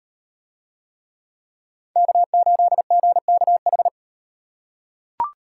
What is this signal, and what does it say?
Morse code, heard as a keyed tone, about 37 wpm, 700 Hz. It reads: K8GKH